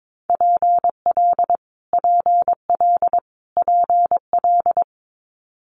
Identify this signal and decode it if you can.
Morse code, heard as a keyed tone, about 22 wpm, 700 Hz. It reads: PL PL PL
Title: Multi-letter Phonemes Repeated 3x 22wpm